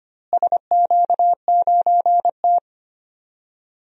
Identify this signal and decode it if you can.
Morse code, heard as a keyed tone, about 25 wpm, 700 Hz. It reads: SQ9T
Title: DXCC-100 Call Signs - 4 Characters - Repeated 2x - 25wpm